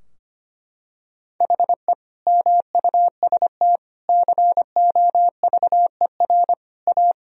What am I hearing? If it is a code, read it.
Morse code, heard as a keyed tone, about 25 wpm, 700 Hz. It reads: HE MUST COVER A
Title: Sentences from Top 1000 Words - Review Part 3 - 25wpm